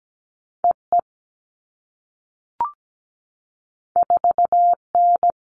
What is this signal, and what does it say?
Morse code, heard as a keyed tone, about 17 wpm, 700 Hz. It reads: EE 4N
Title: All Cut Numbers - 2-Digit 17wpm